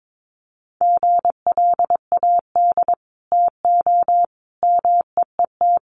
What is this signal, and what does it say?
Morse code, heard as a keyed tone, about 22 wpm, 700 Hz. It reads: GLAD TO MEET
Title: Sentences from Top 1100 Words 22wpm